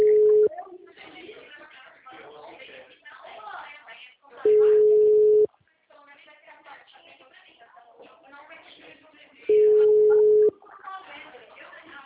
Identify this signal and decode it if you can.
Ringback tone (Europe, 425 Hz), with speech.